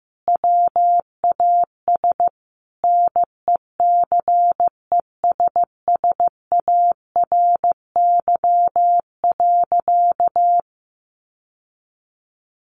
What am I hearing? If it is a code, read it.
Morse code, heard as a keyed tone, about 15 wpm, 700 Hz. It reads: WAS NECESSARY.
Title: Sentences from Top 800 Words 15wpm